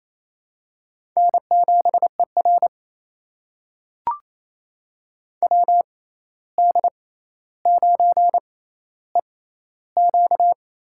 Morse code, heard as a keyed tone, about 28 wpm, spaced as for 13 wpm, 700 Hz. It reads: N7ER WD9EQ
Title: USA Call Signs - 6x Character Spacing - 28wpm